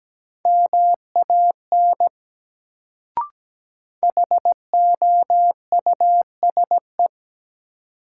Morse code, heard as a keyed tone, about 17 wpm, 700 Hz. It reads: MAN HOUSE
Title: Sentences from Top 200 Words - New Word Drill - 17wpm